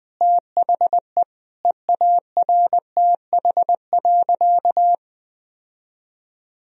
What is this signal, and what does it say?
Morse code, heard as a keyed tone, about 20 wpm, 700 Hz. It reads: THE EARTH.
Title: Sentences from Top 1000 Words - Review Part 3 - 20wpm